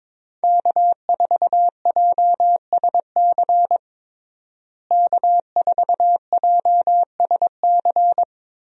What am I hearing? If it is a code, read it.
Morse code, heard as a keyed tone, about 22 wpm, 700 Hz. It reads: K4JSC K4JSC
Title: USA Call Signs - 5 Characters - Repeated 3x - 22wpm